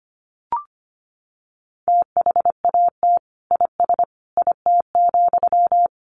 Morse code, heard as a keyed tone, about 25 wpm, 700 Hz. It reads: THAT IS IT,
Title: Sentences from Top 800 Words 25wpm